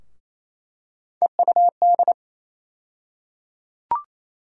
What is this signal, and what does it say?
Morse code, heard as a keyed tone, about 28 wpm, 700 Hz. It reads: EUD
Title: USA Call Sign Postfixes - 3 Letters 28wpm